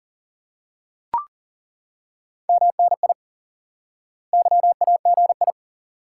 Morse code, heard as a keyed tone, about 40 wpm, 700 Hz. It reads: MNI YAGI
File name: IQE Course 3 - Lesson 89 - YAGI - Two Elements - 40wpm